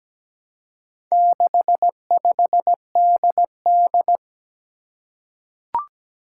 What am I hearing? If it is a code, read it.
Morse code, heard as a keyed tone, about 17 wpm, 700 Hz. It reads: 65DD
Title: Four Challenging Letter-Number Combinations 17wpm